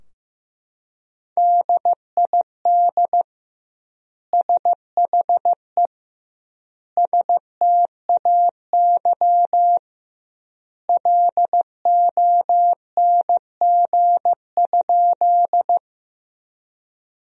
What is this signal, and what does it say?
Morse code, heard as a keyed tone, about 16 wpm, 700 Hz. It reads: DID SHE STAY LONG?
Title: 2x Word-Spacing - Sentences from Top 500 Words - 15wpm